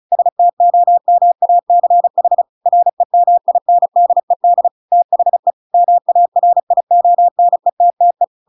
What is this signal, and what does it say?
Morse code, heard as a keyed tone, about 34 wpm, 700 Hz. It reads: STOMACH REMINDED THE MARIONETTE
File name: Pinocchio / Pinocchio ch5 35 35 ik1hge